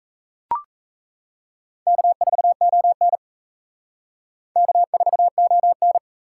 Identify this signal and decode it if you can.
Morse code, heard as a keyed tone, about 35 wpm, 700 Hz. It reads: K4ON K4ON